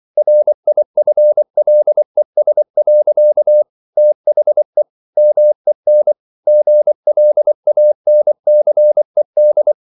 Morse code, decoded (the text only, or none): RIFLES. THE MEN GLANCED